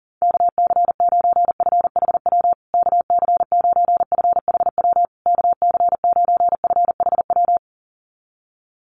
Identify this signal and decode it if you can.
Morse code, heard as a keyed tone, about 40 wpm, 700 Hz. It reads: KC9FHW KC9FHW KC9FHW
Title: USA Call Signs Repeated 3x 40wpm